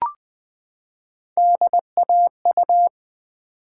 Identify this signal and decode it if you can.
Morse code, heard as a keyed tone, about 20 wpm, 700 Hz. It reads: DAU
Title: Build A Word - 3 Letters At A Time - 8-Letter Words - 20wpm